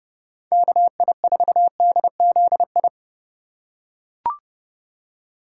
Morse code, heard as a keyed tone, about 32 wpm, 700 Hz. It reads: KI4DZI
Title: USA Call Signs - 6 Characters - 3x Character Spacing - 30wpm